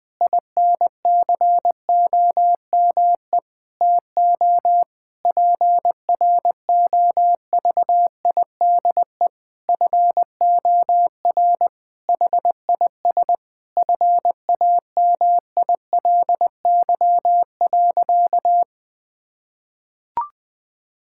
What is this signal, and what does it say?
Morse code, heard as a keyed tone, about 20 wpm, 700 Hz. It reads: INCOME TO PROVIDE FOR HIS FAMILY.
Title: Sentences from Top 1000 Words - Review Part 5 - 20wpm